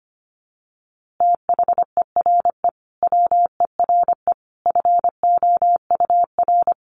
Morse code, heard as a keyed tone, about 25 wpm, 700 Hz. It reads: THERE WERE FOUR
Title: A Sound Of Thunder 25wpm